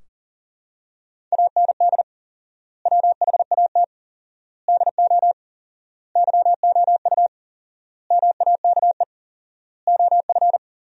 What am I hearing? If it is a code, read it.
Morse code, heard as a keyed tone, about 37 wpm, 700 Hz. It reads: AND WHAT DO YOU MAKE OF
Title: 4x Word-Spacing - Sentences from Top 100 Words - 40wpm